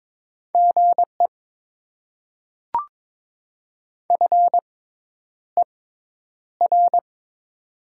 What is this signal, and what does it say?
Morse code, heard as a keyed tone, about 22 wpm, spaced as for 10 wpm, 700 Hz. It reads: GE FER
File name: Top 20 QSO Elements - 6x Character Spacing - 22wpm